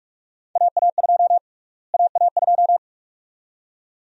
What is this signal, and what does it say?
Morse code, heard as a keyed tone, about 43 wpm, 700 Hz. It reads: AA2 AA2
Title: USA Call Sign Prefixes - Repeated 2x - 45wpm